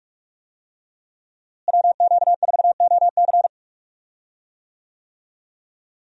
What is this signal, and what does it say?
Morse code, heard as a keyed tone, about 46 wpm, 700 Hz. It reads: WQ4OC
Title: USA Call Signs - 5 Characters - Repeated 3x - 45wpm